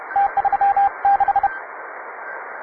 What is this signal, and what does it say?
Morse code, heard as a keyed tone, about 29 wpm, 800 Hz, with radio noise. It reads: T36
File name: wae / cw advanced / tu2ci cw